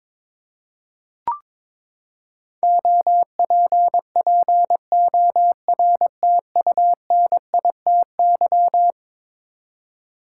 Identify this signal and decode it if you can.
Morse code, heard as a keyed tone, about 22 wpm, 700 Hz. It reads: OPPORTUNITY
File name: practice / 22wpm / Eleven Letter Words - 22wpm